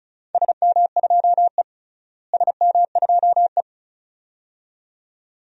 Morse code, heard as a keyed tone, about 34 wpm, 690 Hz. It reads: SM2E SM2E
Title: DXCC-100 Call Signs - 4 Characters - Repeated 2x - 35wpm